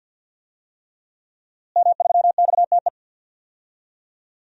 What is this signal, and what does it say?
Morse code, heard as a keyed tone, about 50 wpm, 700 Hz. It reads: M3XTE
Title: DXCC-100 Call Signs - 5 Characters - Repeated 3x - 50wpm